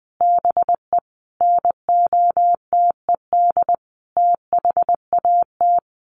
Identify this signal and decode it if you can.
Morse code, heard as a keyed tone, about 20 wpm, 700 Hz. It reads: BE NOTED THAT
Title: Sets of 5 Words 20wpm